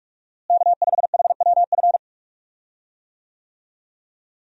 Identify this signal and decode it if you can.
Morse code, heard as a keyed tone, about 46 wpm, 700 Hz. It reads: K5HWF
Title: USA Call Signs - 5 Characters - Repeated 3x - 45wpm